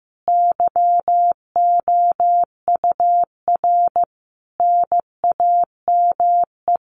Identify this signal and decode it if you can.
Morse code, heard as a keyed tone, about 15 wpm, 700 Hz. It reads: YOUR NAME
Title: Common QSO Phrases 15wpm